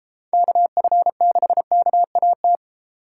Morse code, heard as a keyed tone, about 31 wpm, 700 Hz. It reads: KF6KAT